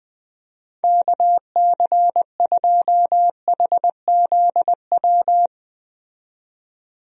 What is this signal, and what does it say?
Morse code, heard as a keyed tone, about 20 wpm, 700 Hz. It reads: KC2HZW
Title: USA Call Signs - 6 Characters - 20wpm